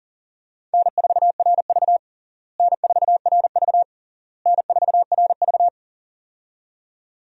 Morse code, heard as a keyed tone, about 40 wpm, 700 Hz. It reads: N4RV N4RV N4RV
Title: USA Call Signs - 4 Characters - Repeated 3x - 40wpm